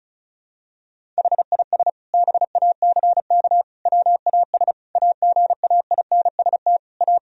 Morse code, heard as a keyed tone, about 36 wpm, 700 Hz. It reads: HIS BACK WAS AGAINST A
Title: Sentences from Top 1000 Words - Review Part 1 - 35wpm